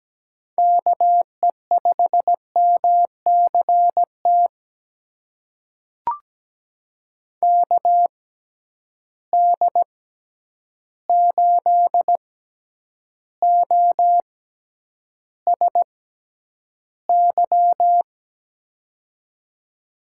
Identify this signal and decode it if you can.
Morse code, heard as a keyed tone, about 17 wpm, spaced as for 9 wpm, 700 Hz. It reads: KE5MCT KD8OSY
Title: USA Call Signs - 6x Character Spacing - 17wpm